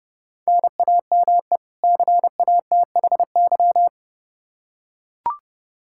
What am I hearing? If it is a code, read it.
Morse code, heard as a keyed tone, about 30 wpm, 700 Hz. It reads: NAME CATHY
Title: Practice QSO - Exchange 1 - Beginning - 30wpm